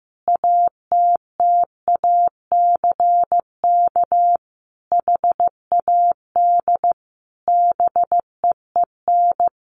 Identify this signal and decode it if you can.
Morse code, heard as a keyed tone, about 15 wpm, 700 Hz. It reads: ATTACK HAD BEEN
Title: Sentences from Top 1500 Words 15wpm